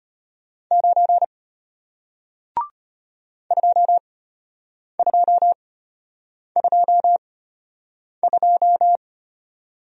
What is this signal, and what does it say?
Morse code, heard as a keyed tone, about 33 wpm, spaced as for 8 wpm, 700 Hz. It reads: 9 2222